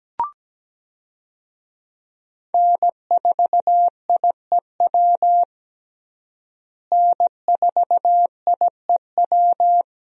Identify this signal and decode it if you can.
Morse code, heard as a keyed tone, about 17 wpm, 700 Hz. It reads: N4IEW N4IEW
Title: USA Call Signs - 5 Characters - Repeated 2x - 17wpm